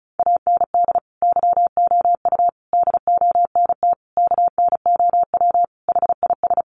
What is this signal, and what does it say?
Morse code, heard as a keyed tone, about 34 wpm, 700 Hz. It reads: AND YOU DONT KNOW HIS
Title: Sentences from Top 1000 Words 35wpm